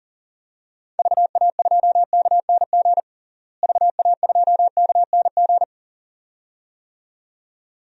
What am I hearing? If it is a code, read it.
Morse code, heard as a keyed tone, about 37 wpm, 700 Hz. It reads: VA2KNG VA2KNG
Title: DXCC-100 Call Signs - 6 Characters - Repeated 2x - 40wpm